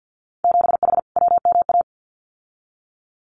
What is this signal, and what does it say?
Morse code, heard as a keyed tone, about 52 wpm, 700 Hz. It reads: 7H WMA
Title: Winter Field Day Exchanges Encore 50wpm